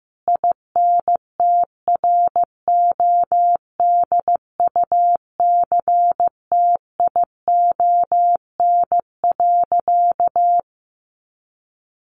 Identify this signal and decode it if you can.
Morse code, heard as a keyed tone, about 15 wpm, 700 Hz. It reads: INTRODUCTION.
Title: Sentences from Top 1000 Words 15wpm